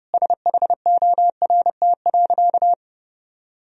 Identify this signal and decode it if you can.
Morse code, heard as a keyed tone, about 30 wpm, 700 Hz. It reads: SHORT.
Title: Sentences from Top 500 Words and Abbreviations 30wpm